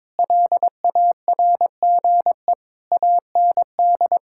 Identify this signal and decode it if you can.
Morse code, heard as a keyed tone, about 22 wpm, 700 Hz. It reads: LARGE AND
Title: Sentences from Top 1000 Words - Review Part 5 - 22wpm